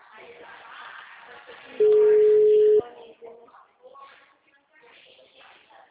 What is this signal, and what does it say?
Ringback tone (Europe, 425 Hz), with speech.
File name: out- -2029- - - .23048